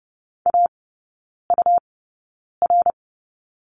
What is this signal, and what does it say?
Morse code, heard as a keyed tone, about 30 wpm, spaced as for 9 wpm, 690 Hz. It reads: AUR
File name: Lesson 17e - Letter u - Three Characters 30wpm